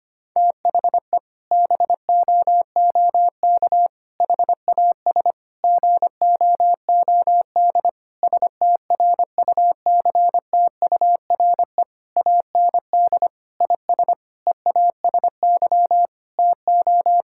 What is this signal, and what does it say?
Morse code, heard as a keyed tone, about 25 wpm, 700 Hz. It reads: THE BOOK HAS GOOD STRUCTURE AND IS EASY TO
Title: Sentences from Top 1000 Words - Review Part 6 - 25wpm